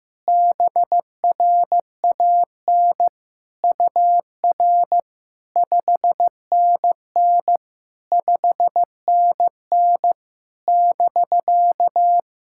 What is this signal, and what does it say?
Morse code, heard as a keyed tone, about 15 wpm, 690 Hz. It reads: BRAN UR 5NN 5NN <BK>
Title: SOTA QSO Practice 15wpm